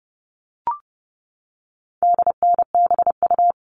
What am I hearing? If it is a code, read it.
Morse code, heard as a keyed tone, about 29 wpm, 700 Hz. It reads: DNBU
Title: All Cut Numbers - 4-Digit 30wpm